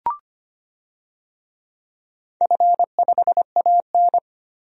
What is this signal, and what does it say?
Morse code, heard as a keyed tone, about 25 wpm, 700 Hz. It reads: F5AN
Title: DXCC-100 Call Signs - 4 Characters - 25wpm